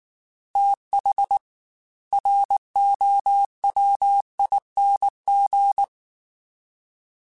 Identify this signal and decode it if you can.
Morse code, heard as a keyed tone, about 19 wpm, 800 Hz. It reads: TH ROWING
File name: TH / TH mp3s / TH Prefixes spt @18